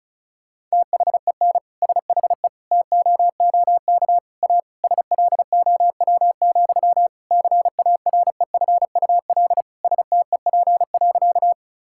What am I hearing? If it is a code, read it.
Morse code, heard as a keyed tone, about 34 wpm, 700 Hz. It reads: THEN SHE TOOK A SLOW, CAREFUL STEP.
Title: Sentences from Top 1600 Words 35wpm